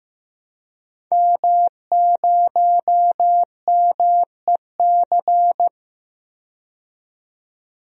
Morse code, heard as a keyed tone, about 15 wpm, 700 Hz. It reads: M0MEC
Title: DXCC-100 Call Signs - 5 Characters - 15wpm